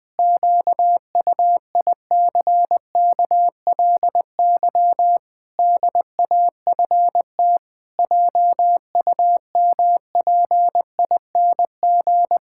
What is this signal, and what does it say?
Morse code, heard as a keyed tone, about 21 wpm, 700 Hz. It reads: QUICKLY DAFT JUMPING
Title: Pangram Challenge v2 - 20wpm